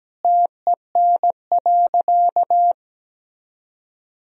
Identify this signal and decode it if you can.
Morse code, heard as a keyed tone, about 17 wpm, 700 Hz. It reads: TEN.
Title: Sentences from Top 1000 Words - Review Part 2 - 17wpm